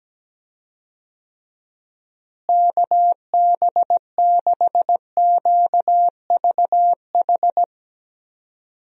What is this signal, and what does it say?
Morse code, heard as a keyed tone, about 17 wpm, 700 Hz. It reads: KB6QVH
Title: USA Call Signs - 6 Characters - Repeated 3x - 17wpm